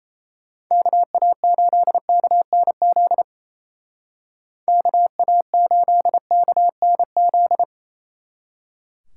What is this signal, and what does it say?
Morse code, heard as a keyed tone, about 37 wpm, 700 Hz. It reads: KA8KNZ KA8KNZ